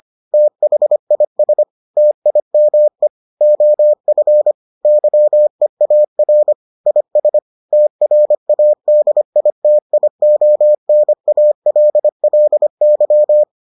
Morse code, heard as a keyed tone, about 25 wpm, 600 Hz. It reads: THIS TIME OF YEAR IS TRADITIONALLY